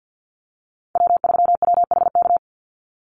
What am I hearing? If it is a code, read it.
Morse code, heard as a keyed tone, about 50 wpm, 700 Hz. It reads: W3WHK
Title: Call Signs - 50wpm